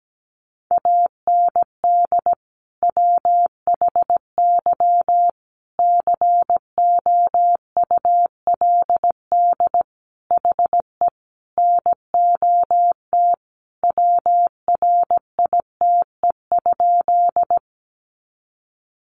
Morse code, heard as a keyed tone, about 17 wpm, 700 Hz. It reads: AND WHY COULD HE NOT WRITE?
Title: Sentences from Top 600 Words 17wpm